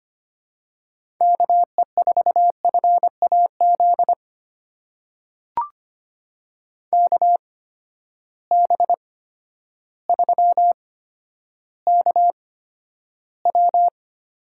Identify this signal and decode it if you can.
Morse code, heard as a keyed tone, about 25 wpm, spaced as for 11 wpm, 700 Hz. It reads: KE4FAZ KB3KW
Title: USA Call Signs - 6 Characters - 8x Character Spacing - 25wpm